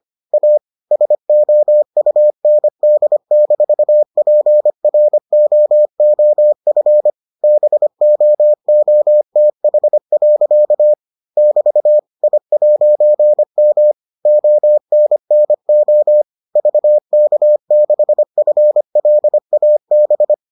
Morse code, heard as a keyed tone, about 25 wpm, 600 Hz. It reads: A SOUND-PROOF BOOTH. <BT> I'M ONNO VK6FLAB